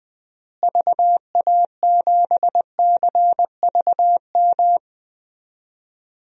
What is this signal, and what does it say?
Morse code, heard as a keyed tone, about 20 wpm, 700 Hz. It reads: VA7CVM
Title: DXCC-100 Call Signs - 6 Characters - Repeated 2x - 20wpm